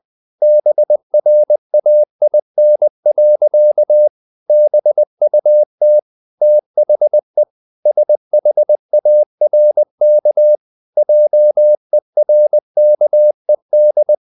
Morse code, heard as a keyed tone, about 20 wpm, 600 Hz. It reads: BRAIN. BUT THE SHARK JERKED
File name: the old man and the sea part 2 20wpm